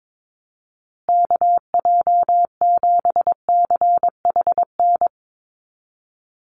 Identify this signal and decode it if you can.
Morse code, heard as a keyed tone, about 22 wpm, 700 Hz. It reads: KJ7CHN
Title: USA Call Signs Repeated 3x 22wpm